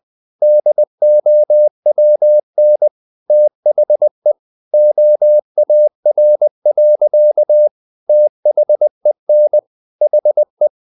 Morse code, heard as a keyed tone, about 20 wpm, 600 Hz. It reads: DOWN THE OAR. THEN HE
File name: the old man and the sea part 2 20wpm